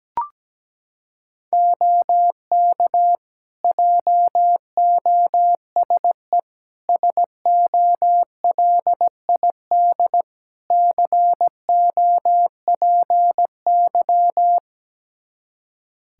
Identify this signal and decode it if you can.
Morse code, heard as a keyed tone, about 17 wpm, 700 Hz. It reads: OK JOSE SOLID COPY
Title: K1USN SST QSO Practice 17wpm